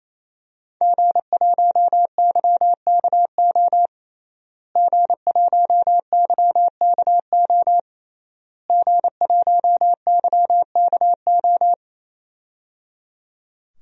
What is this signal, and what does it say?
Morse code, heard as a keyed tone, about 28 wpm, 700 Hz. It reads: G1YKO G1YKO G1YKO
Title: DXCC-100 Call Signs - 5 Characters - Repeated 3x - 28wpm